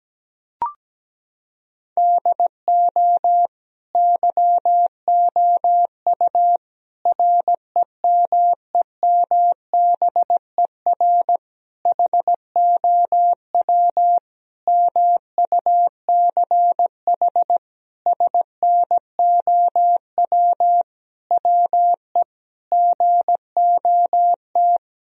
Morse code, heard as a keyed tone, about 17 wpm, 700 Hz. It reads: DO YOU REMEMBER HOW MUCH SNOW WE GOT
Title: Sentences from Top 1000 Words - Review Part 3 - 17wpm